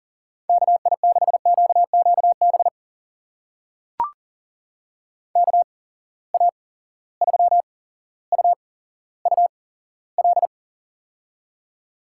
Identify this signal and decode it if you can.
Morse code, heard as a keyed tone, about 39 wpm, spaced as for 17 wpm, 700 Hz. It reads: KI6QQB KA3UUL
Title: USA Call Signs - 8x Character Spacing - 40wpm